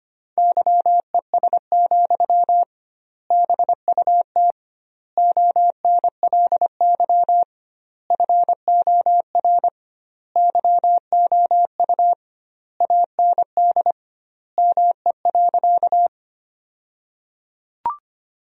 Morse code, heard as a keyed tone, about 25 wpm, 700 Hz. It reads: YES, BUT ONLY FOR YOU AND ME.
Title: 2x Word-Spacing - Sentences from Top 500 Words - 25wpm